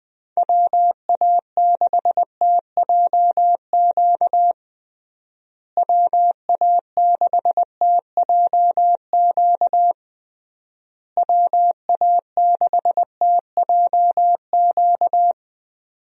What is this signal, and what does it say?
Morse code, heard as a keyed tone, about 20 wpm, 700 Hz. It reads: WA6TJQ WA6TJQ WA6TJQ
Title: USA Call Signs - 6 Characters - Repeated 3x - 20wpm